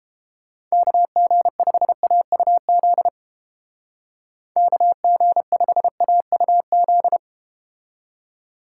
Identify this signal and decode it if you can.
Morse code, heard as a keyed tone, about 32 wpm, 700 Hz. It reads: KG5AUZ KG5AUZ